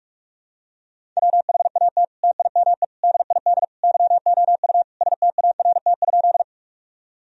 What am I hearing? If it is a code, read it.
Morse code, heard as a keyed tone, about 46 wpm, 700 Hz. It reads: WHAT TIME DID YOU START?
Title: Sentences from Top 1000 Words - Review Part 1 - 45wpm